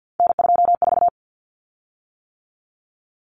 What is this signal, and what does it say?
Morse code, heard as a keyed tone, about 50 wpm, 700 Hz. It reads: N24
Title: Common Cut Numbers - 3-Digit 50wpm